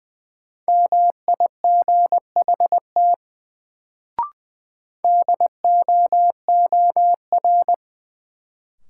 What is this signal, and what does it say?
Morse code, heard as a keyed tone, about 20 wpm, 700 Hz. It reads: MIGHT DOOR
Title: Sentences from Top 300 Words - New Word Drill - 20wpm